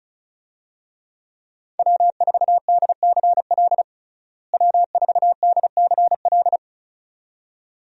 Morse code, heard as a keyed tone, about 34 wpm, 700 Hz. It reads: W4DCL W4DCL
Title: USA Call Signs - 5 Characters - Repeated 2x - 35wpm